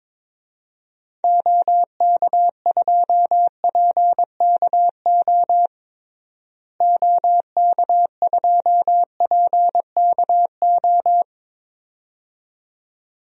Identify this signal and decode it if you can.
Morse code, heard as a keyed tone, about 22 wpm, 700 Hz. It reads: OK2PKO OK2PKO
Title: DXCC-100 Call Signs - 6 Characters - Repeated 2x - 22wpm